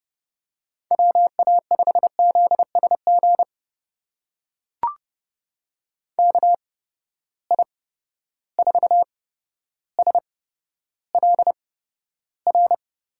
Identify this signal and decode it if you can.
Morse code, heard as a keyed tone, about 31 wpm, spaced as for 13 wpm, 700 Hz. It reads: WA5ZSG KI4SLR
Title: USA Call Signs - 6 Characters - 8x Character Spacing - 30wpm